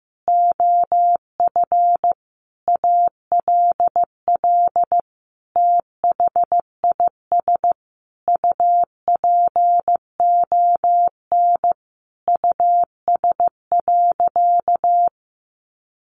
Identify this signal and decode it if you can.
Morse code, heard as a keyed tone, about 15 wpm, 700 Hz. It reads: OF ALL THIS UPON US.
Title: Sentences from Top 1400 Words 15wpm